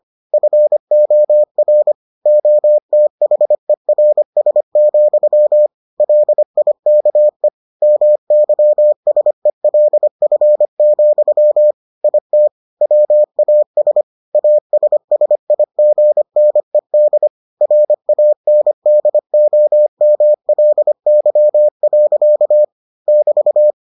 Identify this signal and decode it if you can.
Morse code, heard as a keyed tone, about 25 wpm, 600 Hz. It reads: FOR OTHERS, LIKE MYSELF, IT WAS ASSIGNED RANDOMLY. <BT>